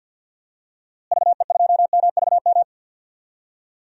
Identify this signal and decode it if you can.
Morse code, heard as a keyed tone, about 47 wpm, 700 Hz. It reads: VE2MVM
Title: DXCC-100 Call Signs - 6 Characters - Repeated 2x - 50wpm